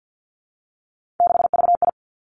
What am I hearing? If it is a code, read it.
Morse code, heard as a keyed tone, about 50 wpm, 700 Hz. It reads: 6VI
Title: Three Challenging Letter-Number Combinations 50wpm